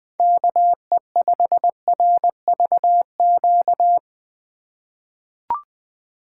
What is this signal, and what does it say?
Morse code, heard as a keyed tone, about 20 wpm, 700 Hz. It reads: KE5RVQ
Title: USA Call Signs - 6 Characters - 4x Character Spacing - 20wpm